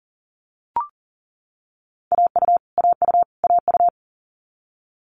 Morse code, heard as a keyed tone, about 40 wpm, 700 Hz. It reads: AU AU AU
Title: Multi-letter Phonemes Repeated 3x 40wpm